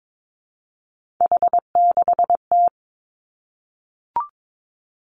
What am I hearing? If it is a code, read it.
Morse code, heard as a keyed tone, about 22 wpm, 700 Hz. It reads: H6T
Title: Three Challenging Letter-Number Combinations 22wpm